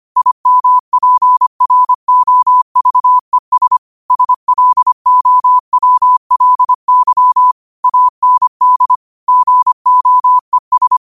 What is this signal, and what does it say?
Morse code, heard as a keyed tone, about 25 wpm, 1000 Hz. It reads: IMPROVES SLOWLY AND GOES
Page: New quotes every day in morse code at 25 Words per minute.